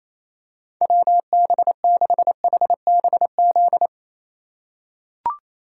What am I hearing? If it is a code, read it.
Morse code, heard as a keyed tone, about 28 wpm, 700 Hz. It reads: WB6HBZ
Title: USA Call Signs - 6 Characters - 3x Character Spacing - 28wpm